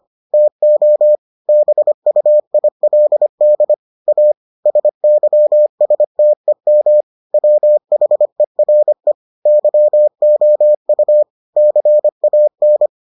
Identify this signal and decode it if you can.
Morse code, heard as a keyed tone, about 25 wpm, 600 Hz. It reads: TO BUILD A SYSTEM WHERE YOU CAN